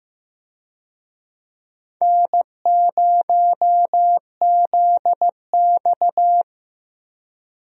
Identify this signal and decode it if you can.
Morse code, heard as a keyed tone, about 15 wpm, 700 Hz. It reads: N0ZX
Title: USA Call Signs - 4 Characters - 15wpm